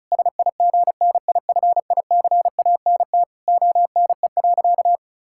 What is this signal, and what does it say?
Morse code, heard as a keyed tone, about 34 wpm, 700 Hz. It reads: SIGNIFICANT ONE.
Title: Sentences from Top 1000 Words - Review Part 6 - 35wpm